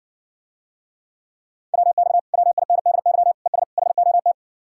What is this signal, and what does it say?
Morse code, heard as a keyed tone, about 60 wpm, 700 Hz. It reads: WX WINDY ES HOT